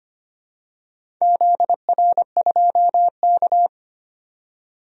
Morse code, heard as a keyed tone, about 25 wpm, 700 Hz. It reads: ZR2K
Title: DXCC-100 Call Signs - 4 Characters - 25wpm